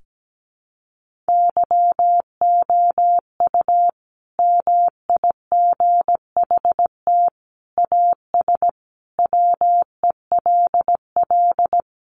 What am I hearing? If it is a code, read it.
Morse code, heard as a keyed tone, about 17 wpm, 700 Hz. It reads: YOU MIGHT AS WELL
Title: Sentences from Top 1400 Words 17wpm